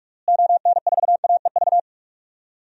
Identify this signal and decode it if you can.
Morse code, heard as a keyed tone, about 46 wpm, 700 Hz. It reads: ON4AEV
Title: DXCC-100 Call Signs - 6 Characters - Repeated 3x - 45wpm